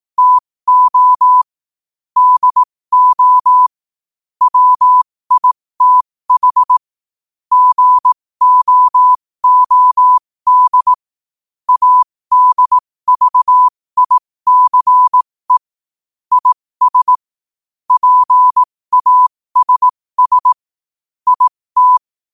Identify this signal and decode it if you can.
Morse code, heard as a keyed tone, about 18 wpm, spaced as for 15 wpm, 1000 Hz. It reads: TO DO WITH GOOD ADVICE IS PASS IT
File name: qotdfiles / 15wpm